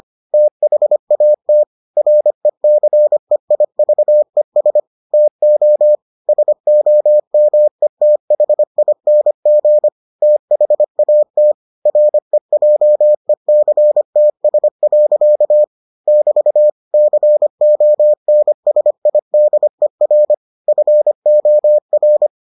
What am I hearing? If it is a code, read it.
Morse code, heard as a keyed tone, about 25 wpm, 600 Hz. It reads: THAT RECEIVES TO SOMETHING THAT REJECTS. <BT> CONSIDER FOR